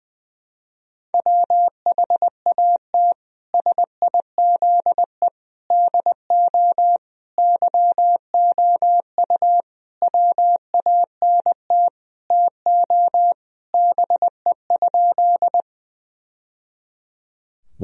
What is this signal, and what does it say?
Morse code, heard as a keyed tone, about 20 wpm, 700 Hz. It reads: WHAT SIZE DO YOU WANT TO BE?
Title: Sentences from Top 1000 Words - Review Part 2 - 20wpm